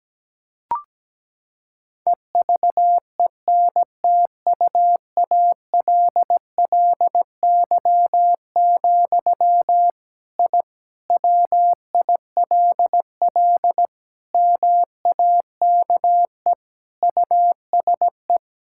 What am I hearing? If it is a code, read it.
Morse code, heard as a keyed tone, about 17 wpm, 700 Hz. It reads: EVENTUALLY, I WILL MAKE USE
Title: Sentences from Top 1000 Words - Review Part 6 - 17wpm